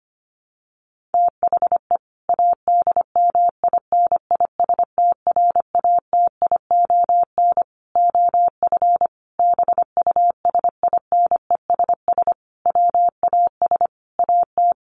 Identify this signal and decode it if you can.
Morse code, heard as a keyed tone, about 25 wpm, 700 Hz. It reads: THE ADMINISTRATION OF BUSINESS WAS AT
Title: Sentences from Top 1300 Words 25wpm